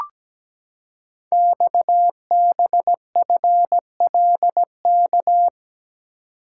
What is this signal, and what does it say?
Morse code, heard as a keyed tone, about 17 wpm, 700 Hz. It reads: XBFLK
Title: Code Groups - 5 Letters - 17wpm